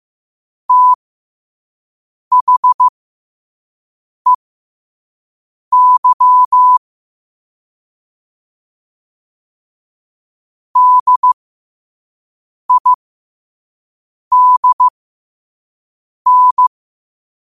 Morse code, heard as a keyed tone, about 15 wpm, spaced as for 5 wpm, 1000 Hz. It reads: THEY DIDN
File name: qotdfiles / 5wpm